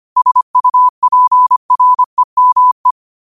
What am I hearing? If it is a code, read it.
Morse code, heard as a keyed tone, about 25 wpm, 1000 Hz. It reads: SUPREME